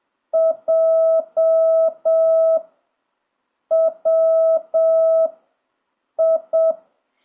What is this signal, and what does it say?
Morse code, heard as a keyed tone, about 7 wpm, 1300 Hz, with noise. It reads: J W I
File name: jh8cbh / 2026cw